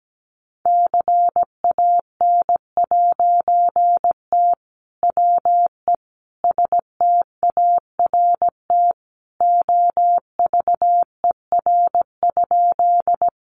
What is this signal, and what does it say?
Morse code, heard as a keyed tone, about 17 wpm, 700 Hz. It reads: CAN'T WE START OVER?
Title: A Sound Of Thunder 17wpm